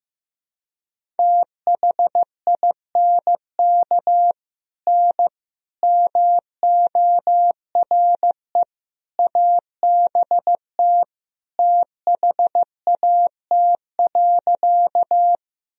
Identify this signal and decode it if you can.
Morse code, heard as a keyed tone, about 15 wpm, 700 Hz. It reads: THINK N MORE ABT THAT.
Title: Sentences from Top 200 Words and Abbreviations 15wpm